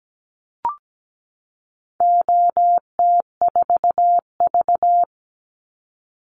Morse code, heard as a keyed tone, about 17 wpm, 700 Hz. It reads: OT4V
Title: Call Signs - 17wpm